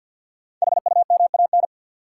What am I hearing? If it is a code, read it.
Morse code, heard as a keyed tone, about 53 wpm, 700 Hz. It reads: HUMAN
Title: Sets of 5 Words - New Word Drill - 50wpm